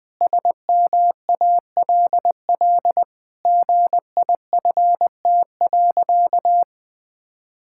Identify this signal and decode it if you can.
Morse code, heard as a keyed tone, about 20 wpm, 700 Hz. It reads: SMALL GIFT.
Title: Sentences from Top 1000 Words - Review Part 6 - 20wpm